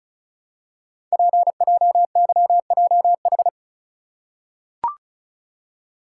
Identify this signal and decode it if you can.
Morse code, heard as a keyed tone, about 36 wpm, 690 Hz. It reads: PJYJH
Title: Code Groups - 5 Letters - 35wpm